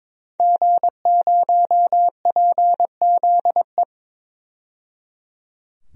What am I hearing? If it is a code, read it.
Morse code, heard as a keyed tone, about 22 wpm, 700 Hz. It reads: G0PZE
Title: DXCC-100 Call Signs - 5 Characters - 22wpm